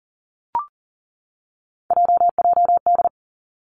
Morse code, heard as a keyed tone, about 38 wpm, 700 Hz. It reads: JJD
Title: Three Challenging Letter-Number Combinations 40wpm